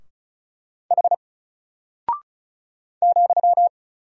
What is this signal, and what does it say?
Morse code, heard as a keyed tone, about 34 wpm, 700 Hz. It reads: H ,
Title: Single Character (Letter:Number:Punctuation:Prosign) 35wpm